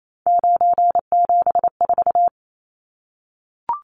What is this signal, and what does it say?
Morse code, heard as a keyed tone, about 28 wpm, 700 Hz. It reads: 974
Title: Numbers - 3-Digit 28wpm